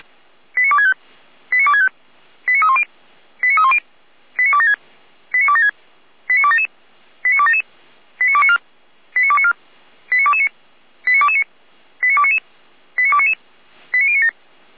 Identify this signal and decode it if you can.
Five-tone selective call (ZVEI), heard as quick runs of five tones, tones of about 70 ms. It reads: 89265 89265 89310 89310 89276 89276 89277 89277 89294 89294 89229 89229 89299 89299 89097